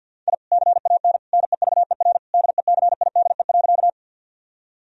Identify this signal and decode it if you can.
Morse code, heard as a keyed tone, about 50 wpm, 700 Hz. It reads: I CAN NEVER DECIDE.
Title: Sentences from Top 1000 Words - Review Part 3 - 50wpm